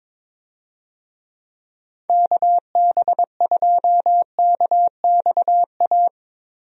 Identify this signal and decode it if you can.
Morse code, heard as a keyed tone, about 22 wpm, 700 Hz. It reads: KB2KXA
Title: USA Call Signs - 6 Characters - Repeated 3x - 22wpm